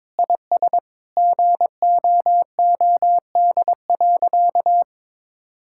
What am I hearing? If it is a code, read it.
Morse code, heard as a keyed tone, about 22 wpm, 700 Hz. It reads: IS GOOD.
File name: practice / 22wpm / Sentences with 3-Word Phrases - 22wpm